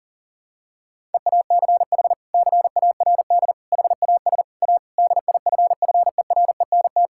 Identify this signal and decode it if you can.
Morse code, heard as a keyed tone, about 40 wpm, 700 Hz. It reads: EACH CARD HAS A DIFFERENT
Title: Sentences from Top 1000 Words - Review Part 6 - 40wpm